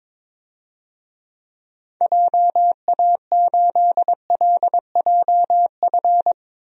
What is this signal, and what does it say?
Morse code, heard as a keyed tone, about 22 wpm, 700 Hz. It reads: JA8LJF
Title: DXCC-100 Call Signs - 6 Characters - Repeated 3x - 22wpm